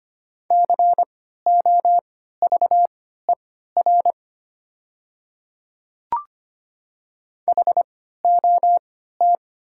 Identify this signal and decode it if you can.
Morse code, heard as a keyed tone, about 25 wpm, spaced as for 14 wpm, 700 Hz. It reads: COVER HOT
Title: 3x Character Spacing - Top 300 Words - 25wpm